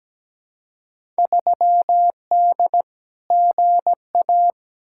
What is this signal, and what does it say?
Morse code, heard as a keyed tone, about 17 wpm, 700 Hz. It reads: 3D GA
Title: Field Day Exchanges Encore - 17wpm